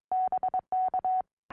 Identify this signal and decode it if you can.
Morse code, heard as a keyed tone, about 22 wpm, 740 Hz. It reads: BK
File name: au / sm6cen 2m